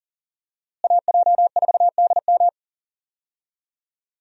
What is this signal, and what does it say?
Morse code, heard as a keyed tone, about 40 wpm, 700 Hz. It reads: AJ4DM
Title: USA Call Signs - 5 Characters - Repeated 2x - 40wpm